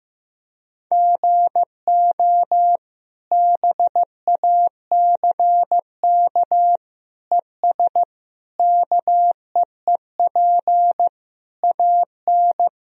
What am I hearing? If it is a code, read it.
Morse code, heard as a keyed tone, about 15 wpm, 700 Hz. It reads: GO BACK ES KEEP AN
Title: Sentences from Top 300 Words and Abbreviations 15wpm